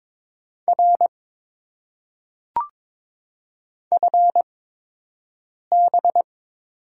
Morse code, heard as a keyed tone, about 22 wpm, spaced as for 6 wpm, 700 Hz. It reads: R FB
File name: Top 90 QSO Elements - 8x Character Spacing - 22wpm